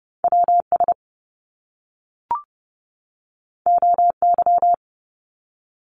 Morse code, heard as a keyed tone, about 30 wpm, 700 Hz. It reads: WS OY
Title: Top Two Letter Combinations in Words 30wpm